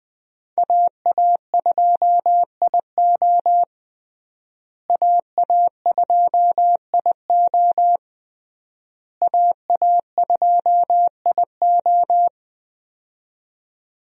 Morse code, heard as a keyed tone, about 21 wpm, 700 Hz. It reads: AA2IO AA2IO AA2IO
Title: USA Call Signs - 5 Characters - Repeated 3x - 20wpm